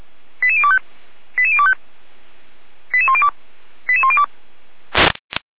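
Five-tone selective call (ZVEI), heard as quick runs of five tones, tones of about 70 ms. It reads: 88025 88025 88102 88102
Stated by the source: ZVEI five-tone alert received by SDR.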